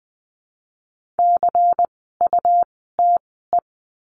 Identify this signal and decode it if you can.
Morse code, heard as a keyed tone, about 20 wpm, spaced as for 14 wpm, 700 Hz. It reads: CUTE
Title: 101 Lesson 17f - Letter u - Words 20wpm